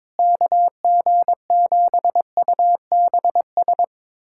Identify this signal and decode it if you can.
Morse code, heard as a keyed tone, about 22 wpm, 700 Hz. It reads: KG7UBS
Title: USA Callsign DE Callsign 22wpm